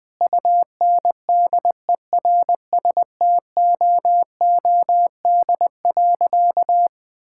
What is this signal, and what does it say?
Morse code, heard as a keyed tone, about 20 wpm, 700 Hz. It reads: UNDERSTOOD.
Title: Sentences with 2-Word Phrases - Spoken Beginning - 20wpm